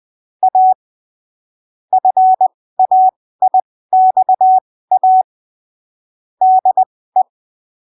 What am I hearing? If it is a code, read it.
Morse code, heard as a keyed tone, about 20 wpm, spaced as for 15 wpm, 750 Hz. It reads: A FAIXA DE
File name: arquivos / texto / texto 12 20ppm